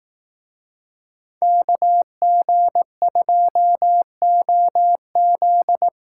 Morse code, heard as a keyed tone, about 18 wpm, 700 Hz. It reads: KG2OZ
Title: Speed Racing - USA 2x2 Call Signs 15wpm